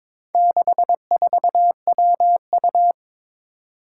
Morse code, heard as a keyed tone, about 22 wpm, 700 Hz. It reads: 64WU
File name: practice / 22wpm / Four Letter-Number Combinations - 22wpm